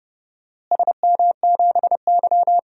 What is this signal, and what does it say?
Morse code, heard as a keyed tone, about 32 wpm, 700 Hz. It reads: SM7Y
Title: DXCC-100 Call Signs - 4 Characters - Repeated 3x - 30wpm